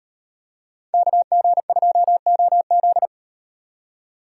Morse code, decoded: KG2OZ

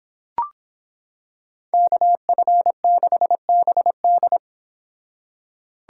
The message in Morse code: KF6BD

